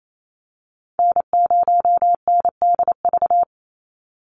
Morse code, heard as a keyed tone, about 28 wpm, 700 Hz. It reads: N0NDV
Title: USA Call Signs 28wpm